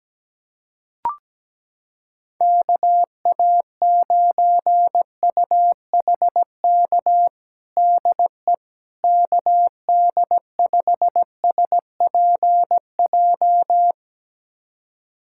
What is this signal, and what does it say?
Morse code, heard as a keyed tone, about 17 wpm, 700 Hz. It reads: KA9UHK DE KD5SPJ
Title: USA Callsign DE Callsign - 6 Characters - 17wpm